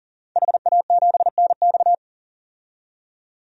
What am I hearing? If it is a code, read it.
Morse code, heard as a keyed tone, about 40 wpm, 700 Hz. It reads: HA7NX
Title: DXCC-100 Call Signs - 5 Characters - 40wpm